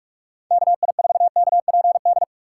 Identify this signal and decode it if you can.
Morse code, heard as a keyed tone, about 46 wpm, 700 Hz. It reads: KI4KPD
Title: USA Call Signs - 6 Characters - 4x Character Spacing - 45wpm